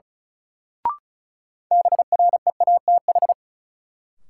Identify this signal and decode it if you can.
Morse code, heard as a keyed tone, about 36 wpm, 700 Hz. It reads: BREATH
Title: Sets of 5 Words Encore - New Word Drill - 35wpm